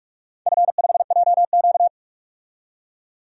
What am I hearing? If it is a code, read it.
Morse code, heard as a keyed tone, about 41 wpm, 700 Hz. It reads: F5JQ
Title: DXCC-100 Call Signs - 4 Characters - 45wpm